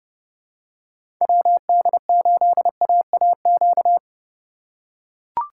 Morse code, heard as a keyed tone, about 30 wpm, 700 Hz. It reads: WD8AAQ
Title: USA Call Signs - 6 Characters - 3x Character Spacing - 30wpm